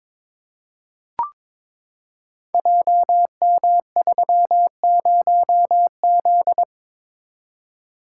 Morse code, heard as a keyed tone, about 22 wpm, 690 Hz. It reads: JM30Z
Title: Code Groups - 5 Letter:Numbers - 22wpm